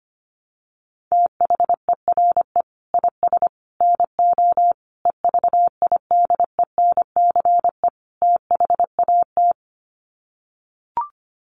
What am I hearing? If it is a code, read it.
Morse code, heard as a keyed tone, about 25 wpm, 700 Hz. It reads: THERE IS NO EVIDENCE THAT
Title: Sets of 5 Words Encore 25wpm